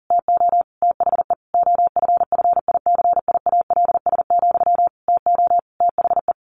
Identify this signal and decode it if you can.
Morse code, heard as a keyed tone, about 37 wpm, 700 Hz. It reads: TO THE OFFICIALS, TO THE
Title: A Sound Of Thunder 40wpm